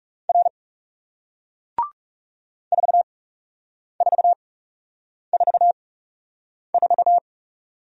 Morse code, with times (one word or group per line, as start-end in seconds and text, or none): 0.3-0.5 R
2.7-7.2 4444